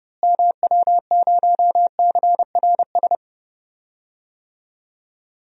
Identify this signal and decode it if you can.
Morse code, heard as a keyed tone, about 29 wpm, 700 Hz. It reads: MW0CRS
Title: DXCC-100 Call Signs - 6 Characters - Repeated 2x - 30wpm